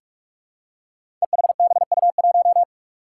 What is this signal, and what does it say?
Morse code, heard as a keyed tone, about 44 wpm, 700 Hz. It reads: EHBU1
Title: Five Challenging Letter-Number Combinations 45wpm